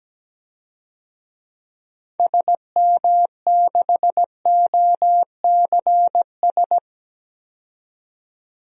Morse code, heard as a keyed tone, about 17 wpm, 700 Hz. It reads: SM6OCS
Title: DXCC-100 Call Signs - 6 Characters - 17wpm